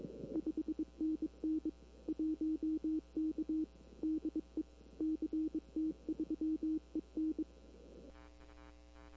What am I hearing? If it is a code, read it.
Morse code, heard as a keyed tone, about 22 wpm, 310 Hz. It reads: 5NN 1K DE CT3EN
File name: CW / 2005-k5tr arrl cw dx -20:00